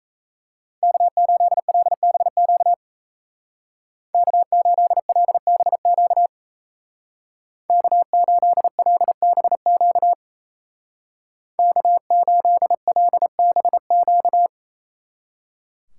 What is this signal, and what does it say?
Morse code, heard as a keyed tone, about 33 wpm, 700 Hz. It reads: K8LBQ K8LBQ K8LBQ K8LBQ
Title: Speed Racing - USA 1x3 Call Signs 28wpm